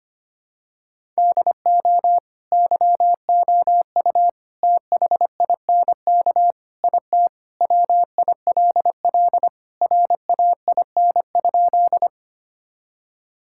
Morse code, heard as a keyed tone, about 25 wpm, 700 Hz. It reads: DO YOU THINK IT WILL RAIN?
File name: practice / 25wpm / Sentences with 4-Word Phrases - 25wpm